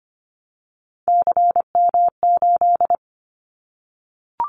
Morse code, heard as a keyed tone, about 25 wpm, 700 Hz. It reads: CM8
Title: Callsign Prefixes for Top 100 DXCC - 25wpm